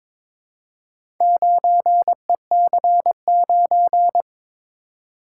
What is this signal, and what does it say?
Morse code, heard as a keyed tone, about 22 wpm, 700 Hz. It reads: 9EC9
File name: practice / 22wpm / Four Letter-Number Combinations - 22wpm